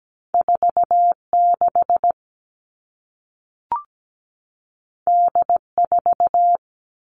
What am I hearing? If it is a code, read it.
Morse code, heard as a keyed tone, about 17 wpm, 700 Hz. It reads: 4 6 D 4
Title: Two Challenging Letter-Number Combinations 17wpm